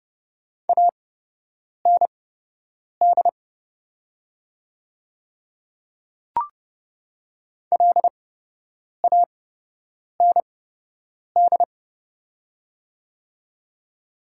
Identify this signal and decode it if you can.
Morse code, heard as a keyed tone, about 30 wpm, spaced as for 8 wpm, 700 Hz. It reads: AND LAND
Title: 8x Character Spacing - Top 500 Words - 30wpm